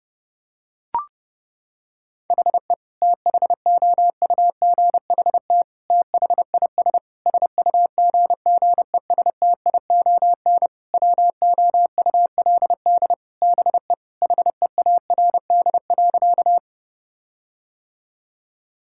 Morse code, heard as a keyed tone, about 32 wpm, 700 Hz. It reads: HE THOUGHT THIS SUGGESTION WOULD BE HEARD.
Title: Sentences from Top 1600 Words 30wpm